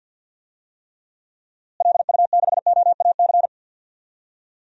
Morse code, heard as a keyed tone, about 46 wpm, 700 Hz. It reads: LU6OAC
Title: DXCC-100 Call Signs - 6 Characters - Repeated 2x - 50wpm